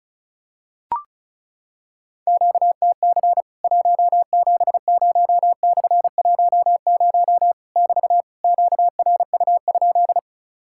Morse code, heard as a keyed tone, about 34 wpm, 700 Hz. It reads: QTC 170/10 <BT> QRU?
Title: Worked All Europe DX Contest - QTC Exchange - 35wpm